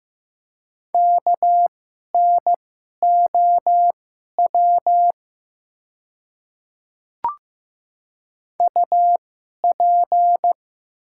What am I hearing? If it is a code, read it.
Morse code, heard as a keyed tone, about 15 wpm, spaced as for 11 wpm, 700 Hz. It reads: KNOW UP
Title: 2x Character Spacing - Top 200 Words - 15wpm